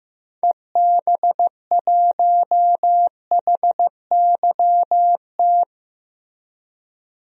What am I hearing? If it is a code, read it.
Morse code, heard as a keyed tone, about 15 wpm, 700 Hz. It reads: EB1HYT
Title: DXCC-100 Call Signs - 6 Characters - Repeated 3x - 15wpm